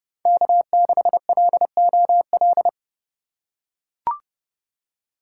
Morse code, heard as a keyed tone, about 30 wpm, 700 Hz. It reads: K6LOL
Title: USA Call Signs - 5 Characters - 2x Character Spacing - 30wpm